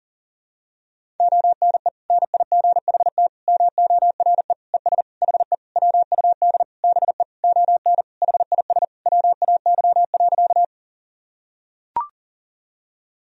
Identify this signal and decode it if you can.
Morse code, heard as a keyed tone, about 40 wpm, 700 Hz. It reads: ONE NIGHT MORE ES HE WUD BE ON HIS WAY.
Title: Sentences from Top 300 Words and Abbreviations 40wpm